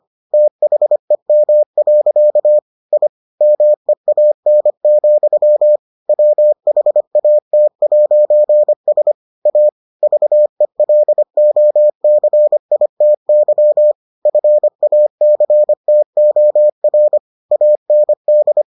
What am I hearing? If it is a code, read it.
Morse code, heard as a keyed tone, about 25 wpm, 600 Hz. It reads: THEM. I MEAN, WHAT'S A VELOCITY FACTOR AND